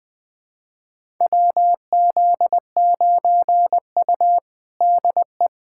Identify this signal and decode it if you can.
Morse code, heard as a keyed tone, about 20 wpm, 700 Hz. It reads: WZ9U DE
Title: USA Callsign DE Callsign - 4 Characters - 20wpm